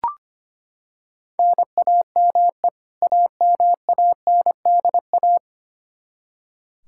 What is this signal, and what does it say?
Morse code, heard as a keyed tone, about 25 wpm, 700 Hz. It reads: NAME AMANDA
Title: Practice QSO - Exchange 1 - Beginning - 25wpm